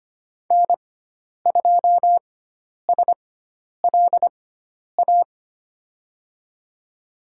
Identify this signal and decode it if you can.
Morse code, heard as a keyed tone, about 25 wpm, spaced as for 10 wpm, 700 Hz. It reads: N2SLA
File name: Lesson 16g - Letter c - Call Signs 25wpm